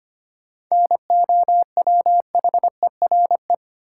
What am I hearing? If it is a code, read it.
Morse code, heard as a keyed tone, about 25 wpm, 700 Hz. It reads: NOWHERE
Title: Sets of 5 Words Encore - New Word Drill - 25wpm